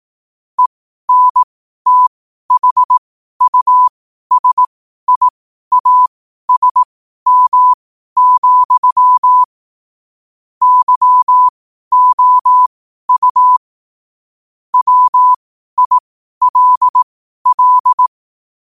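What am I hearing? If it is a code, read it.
Morse code, heard as a keyed tone, about 18 wpm, spaced as for 12 wpm, 1000 Hz. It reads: ENTHUSIASM, YOU WILL
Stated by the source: Morse code at 12 wpm.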